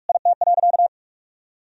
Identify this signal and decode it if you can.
Morse code, heard as a keyed tone, about 46 wpm, 700 Hz. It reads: IT.
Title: Sentences from Top 1000 Words - Review Part 2 - 45wpm